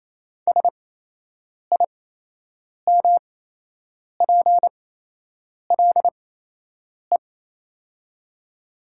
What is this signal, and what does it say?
Morse code, heard as a keyed tone, about 28 wpm, spaced as for 8 wpm, 700 Hz. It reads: SIMPLE
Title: 8x Character Spacing - Top 500 Words - 28wpm